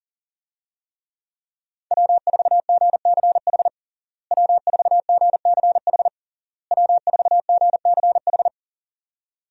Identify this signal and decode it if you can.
Morse code, heard as a keyed tone, about 40 wpm, 700 Hz. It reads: W4GCH W4GCH W4GCH
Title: USA Call Signs - 5 Characters - Repeated 3x - 40wpm